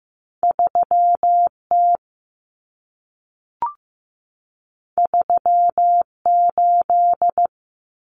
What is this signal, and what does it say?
Morse code, heard as a keyed tone, about 15 wpm, 700 Hz. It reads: 3T 38
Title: Common Cut Numbers - 2-Digit 15wpm